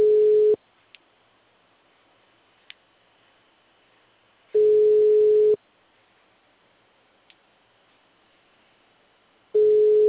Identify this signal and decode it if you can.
Ringback tone (Europe, 425 Hz).
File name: out- -2011- - - .72